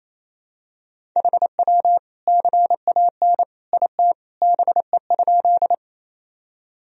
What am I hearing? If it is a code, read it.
Morse code, heard as a keyed tone, about 28 wpm, 700 Hz. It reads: HW CAN IT BE?
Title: Sentences from Top 100 Words and Abbreviations 28wpm